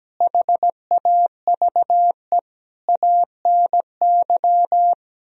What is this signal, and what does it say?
Morse code, heard as a keyed tone, about 17 wpm, 700 Hz. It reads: HAVE ANY
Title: Speed-Racing - Sets of 4 Words 17wpm